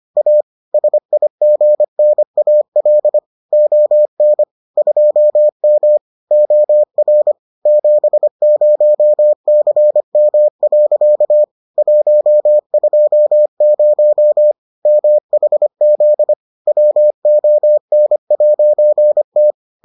Morse code, heard as a keyed tone, about 25 wpm, 600 Hz. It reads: A SIGNAL ON 2M OR 70CM. 120 MHZ WON'T